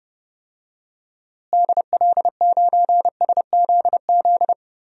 Morse code, heard as a keyed tone, about 30 wpm, 700 Hz. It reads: DL9SZZ
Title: DXCC-100 Call Signs - 6 Characters - Repeated 2x - 30wpm